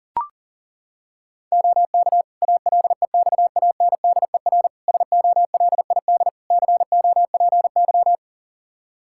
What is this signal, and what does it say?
Morse code, heard as a keyed tone, about 40 wpm, 700 Hz. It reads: OK ALEXANDER SOLID COPY
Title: K1USN SST QSO Practice 40wpm